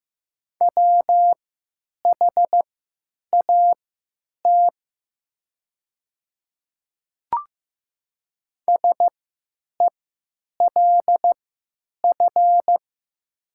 Morse code, heard as a keyed tone, about 15 wpm, spaced as for 9 wpm, 700 Hz. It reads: WHAT SELF
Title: 3x Character Spacing - Top 300 Words - 15wpm